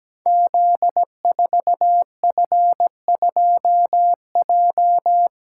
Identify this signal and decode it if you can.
Morse code, heard as a keyed tone, about 17 wpm, 700 Hz. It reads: Z4F2J
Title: Code Groups - 5 Letter:Numbers - 17wpm